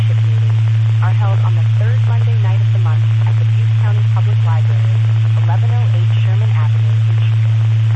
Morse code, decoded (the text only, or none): none